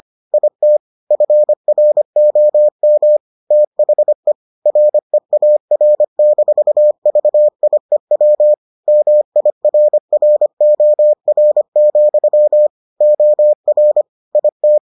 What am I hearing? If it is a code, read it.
Morse code, heard as a keyed tone, about 25 wpm, 600 Hz. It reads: IT FROM THE REAR-VIEW MIRROR, OR IT